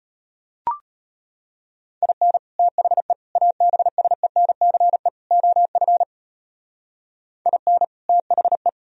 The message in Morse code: IN THE ABSENCE OF IN THE

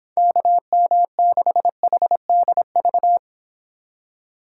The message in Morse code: KM6HDV

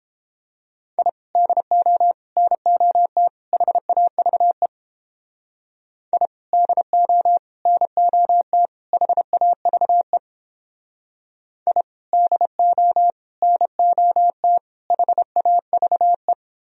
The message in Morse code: I DO NOT HAVE I DO NOT HAVE I DO NOT HAVE

